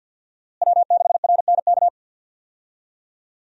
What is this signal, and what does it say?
Morse code, heard as a keyed tone, about 47 wpm, 700 Hz. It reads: W6RNK
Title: USA Call Signs - 5 Characters - Repeated 2x - 50wpm